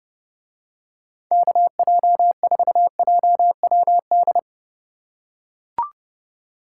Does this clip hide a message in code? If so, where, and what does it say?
Yes: Morse code from 1.3 to 4.4 s, heard as a keyed tone, about 31 wpm, 700 Hz. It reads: KJ4JWD